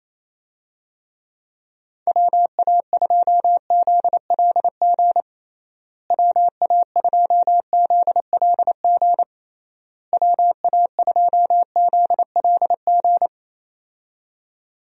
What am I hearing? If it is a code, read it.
Morse code, heard as a keyed tone, about 28 wpm, 700 Hz. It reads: WA2ZLG WA2ZLG WA2ZLG
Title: USA Call Signs - 6 Characters - Repeated 3x - 28wpm